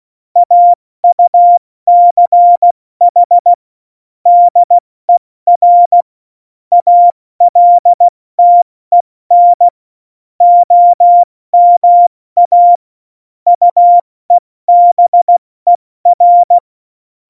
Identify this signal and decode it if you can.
Morse code, heard as a keyed tone, about 16 wpm, 700 Hz. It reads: AUCH DER ALTEN OMA UEBER
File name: Morsekurs / Witze / Witz 30. April 2019, 70 F80